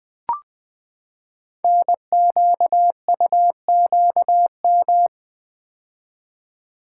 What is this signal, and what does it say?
Morse code, heard as a keyed tone, about 20 wpm, 700 Hz. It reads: NQUQM
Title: Code Groups - 5 Letters - 20wpm